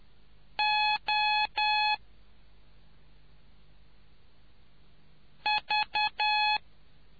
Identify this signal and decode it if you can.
Morse code, heard as a keyed tone, about 10 wpm, 850 Hz. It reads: O V